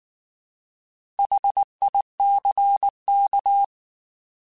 Morse code, heard as a keyed tone, about 19 wpm, 800 Hz. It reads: HICK